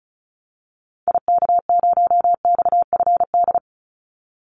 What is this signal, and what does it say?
Morse code, heard as a keyed tone, about 34 wpm, 700 Hz. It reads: IK0XFD
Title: Call Signs Repeated 3x - 35wpm